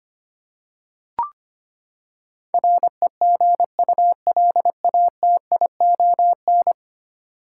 Morse code, heard as a keyed tone, about 25 wpm, 700 Hz. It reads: REGULATION
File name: Ten Letter Words - 25wpm